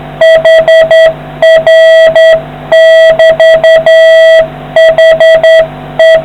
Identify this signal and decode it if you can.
Morse code, heard as a keyed tone, about 10 wpm, 650 Hz, with noise. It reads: HR<BT>HE